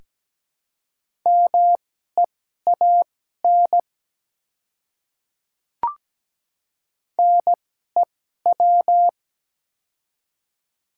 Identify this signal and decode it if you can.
Morse code, heard as a keyed tone, about 17 wpm, spaced as for 12 wpm, 700 Hz. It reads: MEAN NEW
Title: 2x Character Spacing - Top 200 Words - 17wpm